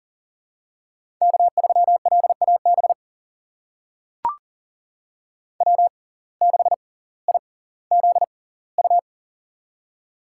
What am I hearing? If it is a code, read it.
Morse code, heard as a keyed tone, about 40 wpm, spaced as for 20 wpm, 700 Hz. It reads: K3LAB W6IZU
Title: USA Call Signs - 5 Characters - 6x Character Spacing - 40wpm